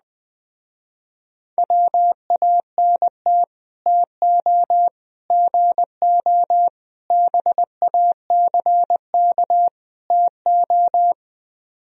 Morse code, heard as a keyed tone, about 20 wpm, 700 Hz. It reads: WANT TO GO BACK TO
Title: Speed-Racing - Sets of 5 Words 17wpm